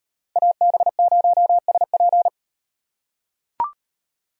Morse code, heard as a keyed tone, about 35 wpm, 700 Hz. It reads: AB0SP